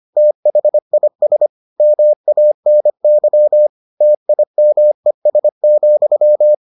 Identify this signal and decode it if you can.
Morse code, heard as a keyed tone, about 25 wpm, 600 Hz. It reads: THIS MANY TIMES,